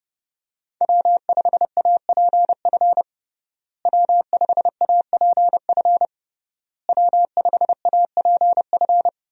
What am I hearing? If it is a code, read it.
Morse code, heard as a keyed tone, about 32 wpm, 700 Hz. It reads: W5APF W5APF W5APF
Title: USA Call Signs - 5 Characters - Repeated 3x - 30wpm